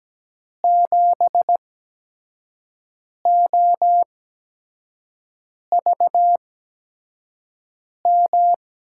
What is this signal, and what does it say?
Morse code, heard as a keyed tone, about 17 wpm, spaced as for 5 wpm, 700 Hz. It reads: 7OVM
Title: USA Call Signs - 6 Characters - 8x Character Spacing - 17wpm